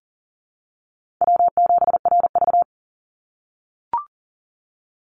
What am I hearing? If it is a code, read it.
Morse code, heard as a keyed tone, about 40 wpm, 700 Hz. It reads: W7RV
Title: NA SOTA Chaser Callsigns 40wpm